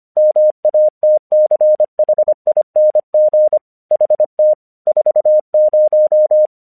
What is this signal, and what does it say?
Morse code, heard as a keyed tone, about 26 wpm, 610 Hz. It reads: MATCHING HT 40
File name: 1 English 25wpm / AQC12